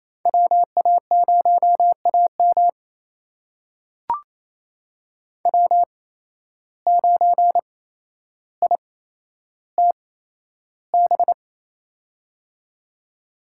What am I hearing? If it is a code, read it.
Morse code, heard as a keyed tone, about 28 wpm, spaced as for 12 wpm, 700 Hz. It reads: WA0AM W9ITB
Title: USA Call Signs - 5 Characters - 8x Character Spacing - 28wpm